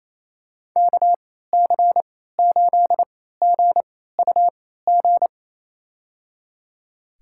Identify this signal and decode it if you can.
Morse code, heard as a keyed tone, about 28 wpm, spaced as for 16 wpm, 700 Hz. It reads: KC8GUG
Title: USA Call Signs - 3x Character Spacing - 28wpm